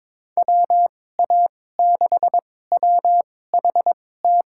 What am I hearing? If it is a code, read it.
Morse code, heard as a keyed tone, about 22 wpm, spaced as for 16 wpm, 700 Hz. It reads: WA6WHT
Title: USA Call Signs - 2x Character Spacing - 22wpm